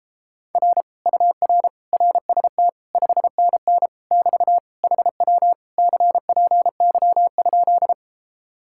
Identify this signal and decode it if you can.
Morse code, heard as a keyed tone, about 33 wpm, 700 Hz. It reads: R UR RST 5NN <BT> HW CPY?